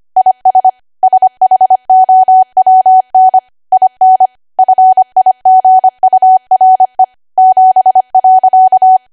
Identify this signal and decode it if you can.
Morse code, heard as a keyed tone, about 25 wpm, 750 Hz. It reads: IS SHOWN IN FIGURE 7.
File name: cwa / W1AW-Level3 / 25WPM